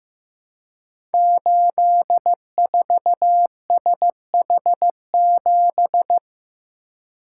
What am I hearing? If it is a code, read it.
Morse code, heard as a keyed tone, about 15 wpm, 700 Hz. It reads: 84SH7
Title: Code Groups - 5 Letter:Numbers - 15wpm